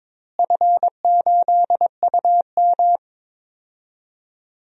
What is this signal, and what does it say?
Morse code, heard as a keyed tone, about 22 wpm, 700 Hz. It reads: F8UM
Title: DXCC-100 Call Signs - 4 Characters - Repeated 2x - 22wpm